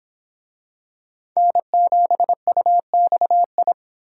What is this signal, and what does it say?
Morse code, heard as a keyed tone, about 25 wpm, 700 Hz. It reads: N7UXI